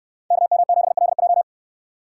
Morse code, heard as a keyed tone, about 69 wpm, 700 Hz. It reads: KM7LJ